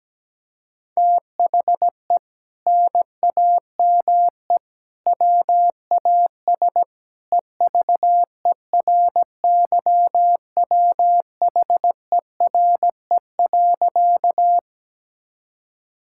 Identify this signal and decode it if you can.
Morse code, heard as a keyed tone, about 17 wpm, 700 Hz. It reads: THE NAME WAS EVERYWHERE.
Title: Sentences from Top 1600 Words 17wpm